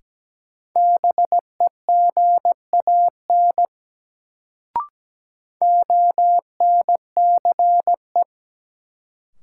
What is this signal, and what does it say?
Morse code, heard as a keyed tone, about 17 wpm, 700 Hz. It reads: BEGAN ONCE
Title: Sentences from Top 300 Words - New Word Drill - 17wpm